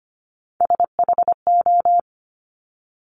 Morse code, heard as a keyed tone, about 25 wpm, 700 Hz. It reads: SHO
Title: Top Three Letter Combinations in Words 25wpm